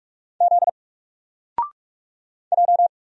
Morse code, heard as a keyed tone, about 44 wpm, 700 Hz. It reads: Z J
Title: Single Letter-Number 45wpm